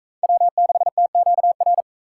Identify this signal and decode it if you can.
Morse code, heard as a keyed tone, about 40 wpm, 700 Hz. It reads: W6TQR